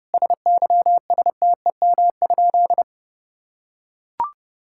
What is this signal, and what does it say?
Morse code, heard as a keyed tone, about 32 wpm, 700 Hz. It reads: SYSTEM?
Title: Sentences from Top 1000 Words - Review Part 5 - 30wpm